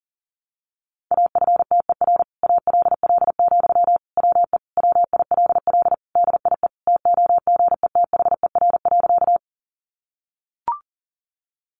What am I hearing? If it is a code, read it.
Morse code, heard as a keyed tone, about 40 wpm, 700 Hz. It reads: AFTER ALL, WE WILL DIE TOGETHER.
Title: Sentences from Top 1300 Words 40wpm